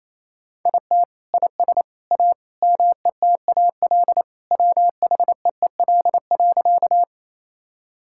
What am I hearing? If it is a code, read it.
Morse code, heard as a keyed tone, about 28 wpm, 700 Hz. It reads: IT IS A METAL WHEEL.
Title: Sentences from Top 1000 Words - Review Part 6 - 28wpm